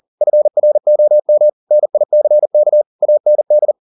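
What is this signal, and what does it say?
Morse code, heard as a keyed tone, about 40 wpm, 600 Hz. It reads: FROM NICK AND
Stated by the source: Morse code at 40 wpm, practice text, CW med Gnister.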